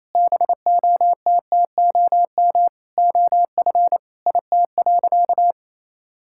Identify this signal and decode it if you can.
Morse code, heard as a keyed tone, about 28 wpm, 700 Hz. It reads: BOTTOM OF IT.
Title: Sentences from Top 1000 Words - Review Part 6 - 28wpm